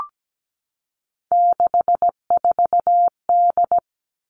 Morse code, heard as a keyed tone, about 17 wpm, 700 Hz. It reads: 64D
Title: Three Challenging Letter-Number Combinations 17wpm